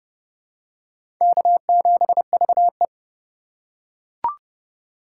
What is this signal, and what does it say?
Morse code, heard as a keyed tone, about 30 wpm, 700 Hz. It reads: K7VE
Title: USA Call Signs - 2x Character Spacing - 30wpm